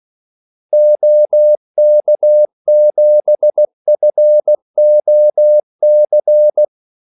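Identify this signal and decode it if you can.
Morse code, heard as a keyed tone, about 16 wpm, 600 Hz. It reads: OK7FOC
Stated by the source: Morse call sign practice at 16 wpm.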